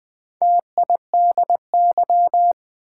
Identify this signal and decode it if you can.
Morse code, heard as a keyed tone, about 20 wpm, 690 Hz. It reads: TIDY
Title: Build A Word - 2 Letters At A Time - 4-Letter Words - 20wpm